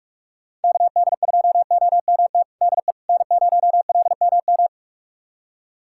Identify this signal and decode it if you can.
Morse code, heard as a keyed tone, about 43 wpm, 700 Hz. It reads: KD2OMT DE N0LMM
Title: USA Callsign DE Callsign 45wpm